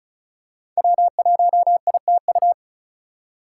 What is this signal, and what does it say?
Morse code, heard as a keyed tone, about 34 wpm, 700 Hz. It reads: W1ITU
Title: USA Call Signs - 4x Character Spacing - 35wpm